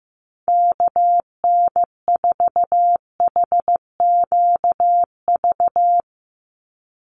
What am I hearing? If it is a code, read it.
Morse code, heard as a keyed tone, about 15 wpm, 700 Hz. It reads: KN4HQV
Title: USA Call Signs 15wpm